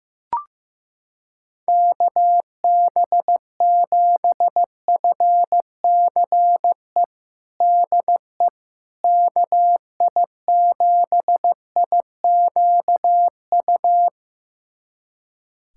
Morse code, heard as a keyed tone, about 15 wpm, 700 Hz. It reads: KB7FCE DE KI7IQU
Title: USA Callsign DE Callsign - 6 Characters - 15wpm